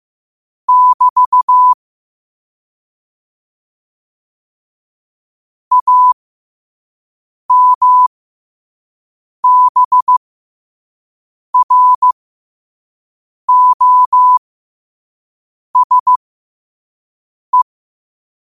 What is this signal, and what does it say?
Morse code, heard as a keyed tone, about 15 wpm, spaced as for 5 wpm, 1000 Hz. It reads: <BT> AMBROSE